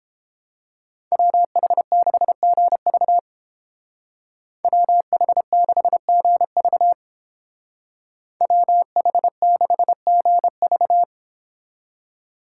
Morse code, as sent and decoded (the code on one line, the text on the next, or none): .-- .... -.... --. ...- / .-- .... -.... --. ...- / .-- .... -.... --. ...-
WH6GV WH6GV WH6GV